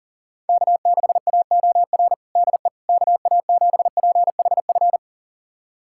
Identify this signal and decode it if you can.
Morse code, heard as a keyed tone, about 40 wpm, 700 Hz. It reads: K6AOR DE KA7PHF
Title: USA Callsign DE Callsign 40wpm